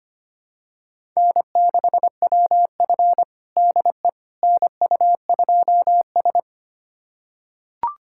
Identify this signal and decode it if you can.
Morse code, heard as a keyed tone, about 25 wpm, 700 Hz. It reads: N6WF DE NU2S
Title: USA Callsign DE Callsign - 4 Characters - 25wpm